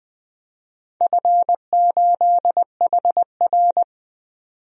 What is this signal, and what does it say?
Morse code, heard as a keyed tone, about 21 wpm, 700 Hz. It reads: F8HR
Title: DXCC-100 Call Signs - 4 Characters - 20wpm